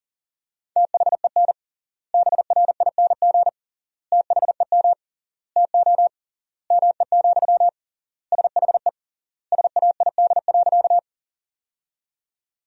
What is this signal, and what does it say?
Morse code, heard as a keyed tone, about 37 wpm, 700 Hz. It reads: THEN BRING THEM TO ME, SHE SAID.
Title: 3x Word-Spacing - Sentences from Top 500 Words - 40wpm